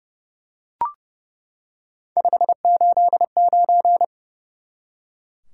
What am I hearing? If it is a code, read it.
Morse code, heard as a keyed tone, about 30 wpm, 700 Hz. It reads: 589
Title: Top 180 QSO Elements 30wpm